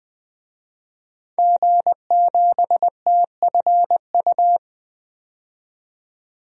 Morse code, heard as a keyed tone, about 20 wpm, 700 Hz. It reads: G7TFU
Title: DXCC-100 Call Signs - 5 Characters - 20wpm